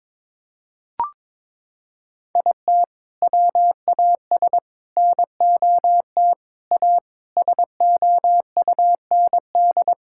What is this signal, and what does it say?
Morse code, heard as a keyed tone, about 22 wpm, 700 Hz. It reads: IT WAS NOT A SOUND
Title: Sentences from Top 1000 Words - Review Part 6 - 22wpm